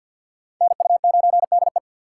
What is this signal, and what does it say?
Morse code, heard as a keyed tone, about 60 wpm, 700 Hz. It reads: NU9DE
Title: Speed Racing - USA 2x2 Call Signs 50wpm